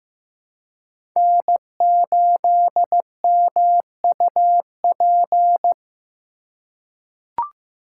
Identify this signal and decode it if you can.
Morse code, heard as a keyed tone, about 15 wpm, 700 Hz. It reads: N8MUP
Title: USA Call Signs - 5 Characters - 2x Character Spacing - 15wpm